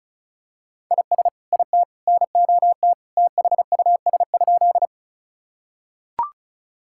Morse code, heard as a keyed tone, about 36 wpm, 700 Hz. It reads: IS IT NOT THUS?
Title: Sentences from Top 1000 Words - Review Part 5 - 35wpm